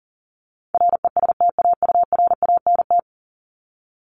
Morse code, heard as a keyed tone, about 40 wpm, 700 Hz. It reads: RESTAURANT
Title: Top 1751-2000 Words 40wpm